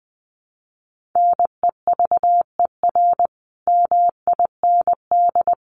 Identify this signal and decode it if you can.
Morse code, heard as a keyed tone, about 20 wpm, 700 Hz. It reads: NEVER MIND
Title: Sentences from Top 500 Words 20wpm